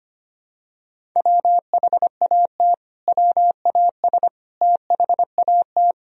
Morse code, heard as a keyed tone, about 25 wpm, 700 Hz. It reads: WHAT WAS THAT
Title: Sentences from Top 1000 Words - Review Part 1 - 25wpm